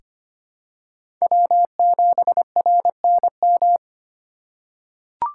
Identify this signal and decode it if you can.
Morse code, heard as a keyed tone, about 25 wpm, 700 Hz. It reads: W7RNM
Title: USA Call Signs - 5 Characters - 2x Character Spacing - 25wpm